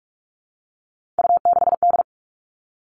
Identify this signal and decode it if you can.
Morse code, heard as a keyed tone, about 44 wpm, 700 Hz. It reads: U6D
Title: All Cut Numbers - 3-Digit 45wpm